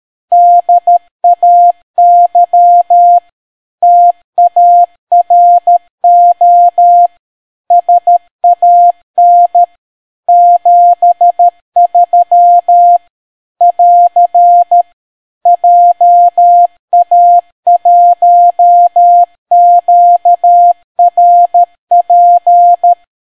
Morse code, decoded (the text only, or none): DAY TARO SAN 73 <AR> JA1QRP